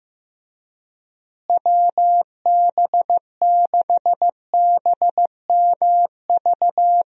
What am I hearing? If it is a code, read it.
Morse code, heard as a keyed tone, about 15 wpm, 700 Hz. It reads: WB6BMV
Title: USA Call Signs - 6 Characters - Repeated 2x - 15wpm